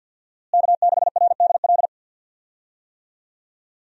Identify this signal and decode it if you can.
Morse code, heard as a keyed tone, about 46 wpm, 700 Hz. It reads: K6RDL
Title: USA Call Signs - 5 Characters - Repeated 3x - 50wpm